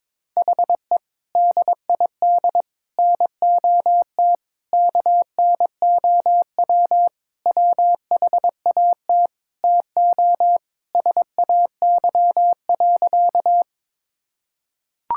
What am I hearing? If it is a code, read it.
Morse code, heard as a keyed tone, about 22 wpm, 700 Hz. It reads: HE DID NOT KNOW WHAT TO SAY.
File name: practice / 22wpm / Sentences with 3-Word Phrases - 22wpm